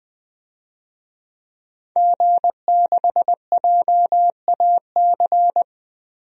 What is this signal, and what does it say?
Morse code, heard as a keyed tone, about 20 wpm, 700 Hz. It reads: G6JAC
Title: DXCC-100 Call Signs - 5 Characters - 20wpm